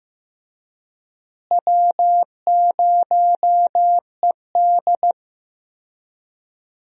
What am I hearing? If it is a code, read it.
Morse code, heard as a keyed tone, about 16 wpm, 700 Hz. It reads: W0ED
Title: USA Call Signs - 4 Characters - Repeated 3x - 15wpm